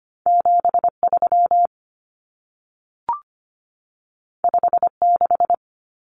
Morse code, heard as a keyed tone, about 25 wpm, 700 Hz. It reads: 73 56
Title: Numbers - 2-Digit 25wpm